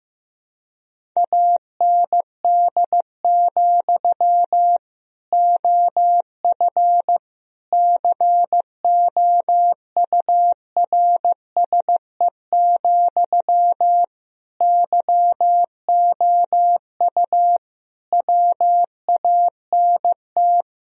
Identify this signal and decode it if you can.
Morse code, heard as a keyed tone, about 15 wpm, 700 Hz. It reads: AND, OF COURSE, YOU WANT
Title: Sentences from Top 1000 Words - Review Part 1 - 15wpm